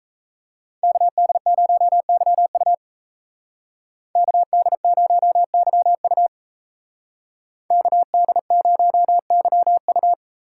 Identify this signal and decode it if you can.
Morse code, heard as a keyed tone, about 34 wpm, 700 Hz. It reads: KD0YU KD0YU KD0YU
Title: Speed Racing - USA 2x2 Call Signs 28wpm